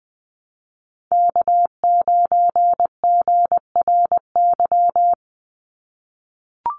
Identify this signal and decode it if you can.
Morse code, heard as a keyed tone, about 20 wpm, 700 Hz. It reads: K9GRY
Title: USA Call Signs 20wpm